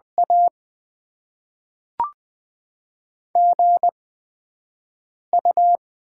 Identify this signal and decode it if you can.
Morse code, heard as a keyed tone, about 20 wpm, 700 Hz. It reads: A G U
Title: Top 20 QSO Elements - 8x Character Spacing - 20wpm